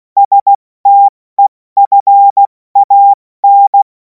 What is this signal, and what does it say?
Morse code, heard as a keyed tone, about 16 wpm, spaced as for 14 wpm, 800 Hz. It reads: STEFAN
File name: Morsekurs / Witze / Witz 07. April 2026 70 F80